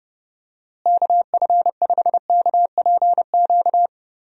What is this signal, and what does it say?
Morse code, heard as a keyed tone, about 30 wpm, 700 Hz. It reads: KF5KPQ
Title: USA Call Signs - 6 Characters - Repeated 2x - 30wpm